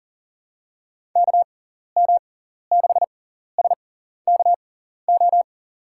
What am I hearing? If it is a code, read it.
Morse code, heard as a keyed tone, about 40 wpm, spaced as for 14 wpm, 700 Hz. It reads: KM6SKO
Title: USA Call Signs - 6 Characters - 6x Character Spacing - 40wpm